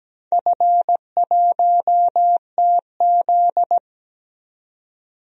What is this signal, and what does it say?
Morse code, heard as a keyed tone, about 17 wpm, 700 Hz. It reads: F1TZ
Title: DXCC-100 Call Signs - 4 Characters - 17wpm